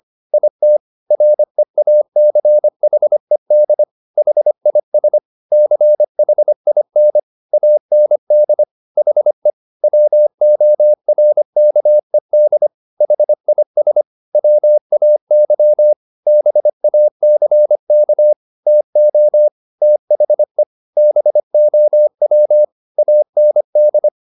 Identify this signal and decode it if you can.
Morse code, heard as a keyed tone, about 25 wpm, 600 Hz. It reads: IT REACHED HIS CHIN AND HE WORKED HIS WAY BACK TO THE BOW AND
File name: the old man and the sea part 1 25wpm